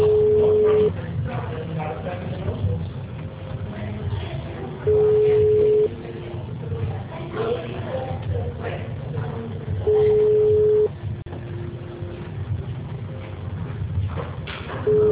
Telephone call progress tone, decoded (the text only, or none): ringback tone